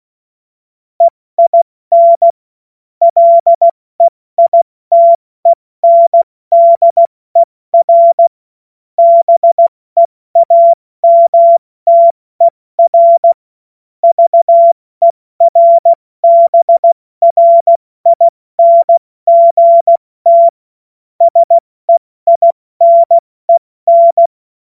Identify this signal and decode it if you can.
Morse code, heard as a keyed tone, about 16 wpm, 680 Hz. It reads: EIN LEITENDER BEAMTER VERBRINGT SEINEN
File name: Witz-16-September-2025-70-F80